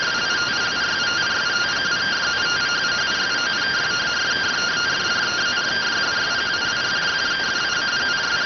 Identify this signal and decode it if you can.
Two-tone teleprinter signal, about 170 Hz shift.